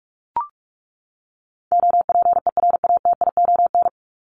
Morse code, heard as a keyed tone, about 44 wpm, 700 Hz. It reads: OPERATION
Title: Top 1501-1750 Words 45wpm